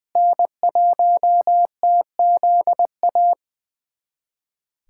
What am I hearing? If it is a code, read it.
Morse code, heard as a keyed tone, about 20 wpm, 700 Hz. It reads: N1TZA
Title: USA Callsign DE Callsign - 5 Characters - 20wpm